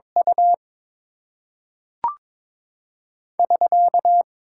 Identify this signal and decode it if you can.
Morse code, heard as a keyed tone, about 22 wpm, 700 Hz. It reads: U <SK>
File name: Top 90 QSO Elements - 8x Character Spacing - 22wpm